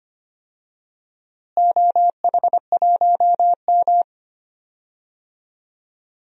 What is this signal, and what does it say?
Morse code, heard as a keyed tone, about 25 wpm, 700 Hz. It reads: OH1M
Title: DXCC-100 Call Signs - 4 Characters - Repeated 2x - 25wpm